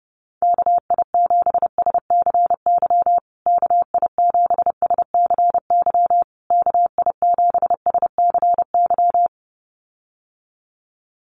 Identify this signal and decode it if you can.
Morse code, heard as a keyed tone, about 28 wpm, 700 Hz. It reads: KI7SCY KI7SCY KI7SCY
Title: USA Call Signs Repeated 3x 30wpm